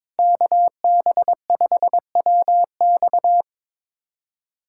Morse code, heard as a keyed tone, about 22 wpm, 700 Hz. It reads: KB5WX
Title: USA Callsign DE Callsign - 5 Characters - 22wpm